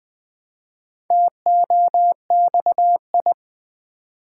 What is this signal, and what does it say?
Morse code, heard as a keyed tone, about 21 wpm, 700 Hz. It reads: TOXI
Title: Build A Word - 1 Letter At A Time - 3 to 5-Letter Words - 20wpm